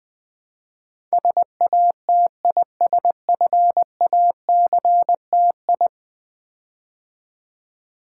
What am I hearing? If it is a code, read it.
Morse code, heard as a keyed tone, about 20 wpm, 700 Hz. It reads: SATISFACTI
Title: Build A Word - 5 Letters At A Time - 12-Letter Words - 20wpm